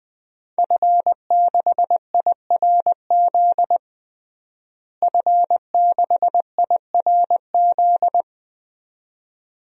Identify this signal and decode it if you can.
Morse code, heard as a keyed tone, about 20 wpm, 700 Hz. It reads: F6IRZ F6IRZ
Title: DXCC-100 Call Signs - 5 Characters - Repeated 2x - 20wpm